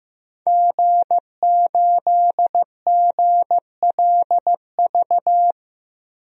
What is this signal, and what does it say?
Morse code, heard as a keyed tone, about 15 wpm, 700 Hz. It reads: G8GLV
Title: DXCC-100 Call Signs - 5 Characters - Repeated 3x - 15wpm